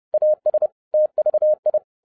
Morse code, heard as a keyed tone, about 30 wpm, 610 Hz. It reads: AS TVI
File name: 2011-07-mp3 / n40 EXCITING QRP CW OPERATION 30wpm